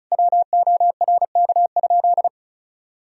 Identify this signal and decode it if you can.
Morse code, heard as a keyed tone, about 36 wpm, 700 Hz. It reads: WORK?
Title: Sentences from Top 500 Words and Abbreviations 35wpm